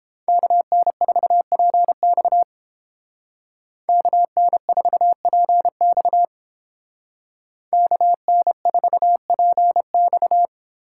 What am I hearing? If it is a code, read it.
Morse code, heard as a keyed tone, about 29 wpm, 700 Hz. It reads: KN4PX KN4PX KN4PX